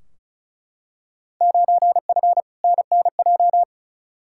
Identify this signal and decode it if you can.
Morse code, heard as a keyed tone, about 34 wpm, 700 Hz. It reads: 9F NNJ
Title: Field Day Exchanges Encore - 35wpm